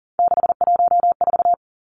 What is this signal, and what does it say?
Morse code, heard as a keyed tone, about 40 wpm, 700 Hz. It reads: 614
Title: Numbers - 3-Digit 40wpm